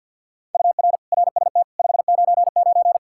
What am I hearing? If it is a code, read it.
Morse code, heard as a keyed tone, about 67 wpm, spaced as for 58 wpm, 700 Hz. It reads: UR RST 599
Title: Speed-Racing Common QSO Phrases Encore - 50wpm